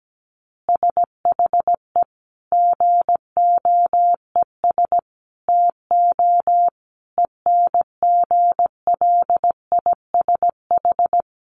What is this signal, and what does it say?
Morse code, heard as a keyed tone, about 17 wpm, 700 Hz. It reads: SHE GOES TO ENGLISH
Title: Additional Easy Phrases 17wpm